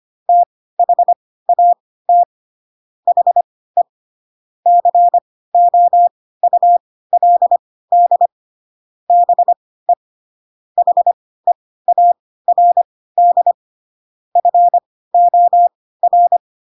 Morse code, heard as a keyed tone, about 25 wpm, spaced as for 16 wpm, 700 Hz. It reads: THAT HE COULD BE HEARD FOR
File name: Pinocchio / Pinocchio ch7 10 25 ik1hge